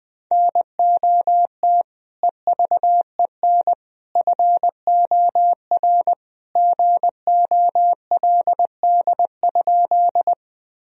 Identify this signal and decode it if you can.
Morse code, heard as a keyed tone, about 20 wpm, 700 Hz. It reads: NOT EVEN FOR GOLD?
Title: Sentences from Top 1000 Words - Review Part 3 - 20wpm